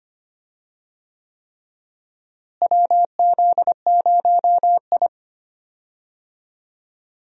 Morse code, heard as a keyed tone, about 25 wpm, 700 Hz. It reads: WZ0I
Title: USA Call Signs - 4 Characters - Repeated 3x - 25wpm